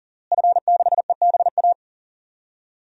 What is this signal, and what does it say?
Morse code, heard as a keyed tone, about 40 wpm, 700 Hz. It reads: F6EBA
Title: DXCC-100 Call Signs - 5 Characters - Repeated 3x - 40wpm